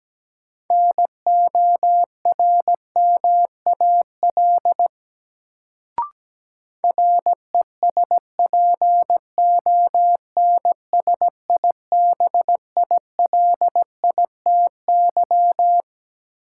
Morse code, heard as a keyed tone, about 17 wpm, 700 Hz. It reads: NORMAL RESPONSIBILITY
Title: Sentences from Top 1200 Words - New Word Drill Alt - 17wpm